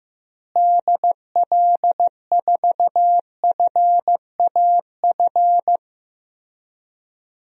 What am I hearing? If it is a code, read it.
Morse code, heard as a keyed tone, about 15 wpm, 700 Hz. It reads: DL4FAF
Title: DXCC-100 Call Signs - 6 Characters - Repeated 3x - 15wpm